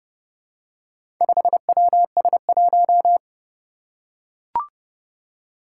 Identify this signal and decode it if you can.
Morse code, heard as a keyed tone, about 29 wpm, 700 Hz. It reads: 5WS1
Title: Four Challenging Letter-Number Combinations 30wpm